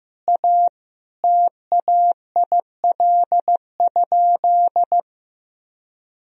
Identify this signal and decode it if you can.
Morse code, heard as a keyed tone, about 16 wpm, 700 Hz. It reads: A TAIL?
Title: Sentences from Top 1000 Words - Review Part 3 - 15wpm